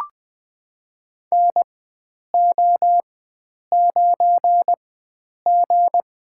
Morse code, heard as a keyed tone, about 20 wpm, spaced as for 9 wpm, 700 Hz. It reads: NO9G
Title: USA Call Signs - 4x Character Spacing - 20wpm